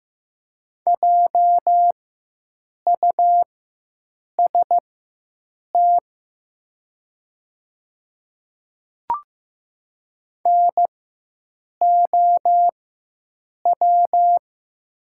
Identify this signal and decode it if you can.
Morse code, heard as a keyed tone, about 15 wpm, spaced as for 7 wpm, 700 Hz. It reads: JUST NOW
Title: 4x Character Spacing - Top 500 Words - 15wpm